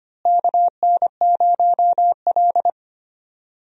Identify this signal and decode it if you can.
Morse code, heard as a keyed tone, about 25 wpm, 700 Hz. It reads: KN0L
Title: USA Call Signs - 4 Characters - Repeated 2x - 25wpm